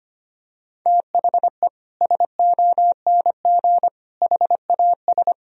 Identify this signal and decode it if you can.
Morse code, heard as a keyed tone, about 25 wpm, 700 Hz. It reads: THE SONG HAS
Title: Sentences from Top 1000 Words - Review Part 3 - 25wpm